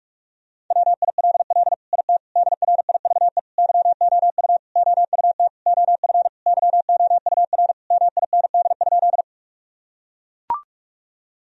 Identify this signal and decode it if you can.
Morse code, heard as a keyed tone, about 41 wpm, 700 Hz. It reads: WILL IT DRIVE YOU OUT OF YOUR MIND?
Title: Sentences from Top 1000 Words - Review Part 2 - 45wpm